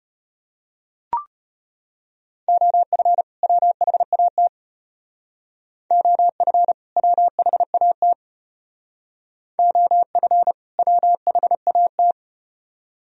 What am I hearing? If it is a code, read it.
Morse code, heard as a keyed tone, about 34 wpm, 700 Hz. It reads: OF WHAT OF WHAT OF WHAT